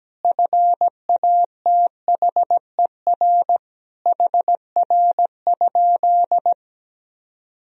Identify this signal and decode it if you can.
Morse code, heard as a keyed tone, about 17 wpm, 700 Hz. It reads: FATHER HR?
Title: Sentences from Top 300 Words and Abbreviations 17wpm